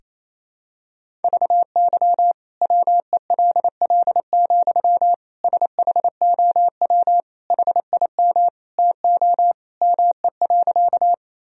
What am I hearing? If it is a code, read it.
Morse code, heard as a keyed tone, about 28 wpm, 700 Hz. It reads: VY WELL, SHOW HIM TO ME.
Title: Sentences from Top 200 Words and Abbreviations 28wpm